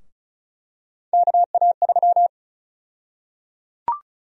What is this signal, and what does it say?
Morse code, heard as a keyed tone, about 36 wpm, 700 Hz. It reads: KA3
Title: USA Call Sign Prefixes - Repeated 2x - 35wpm